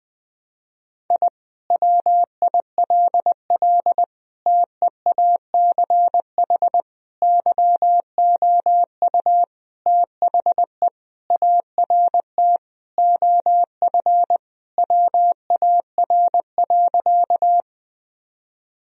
Morse code, heard as a keyed tone, about 20 wpm, 700 Hz. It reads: I WILL TEACH YOU THE ART OF WAR.
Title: Sentences from Top 1000 Words - Review Part 3 - 20wpm